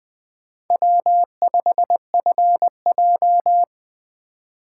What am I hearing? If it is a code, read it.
Morse code, heard as a keyed tone, about 19 wpm, 700 Hz. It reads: W5FJ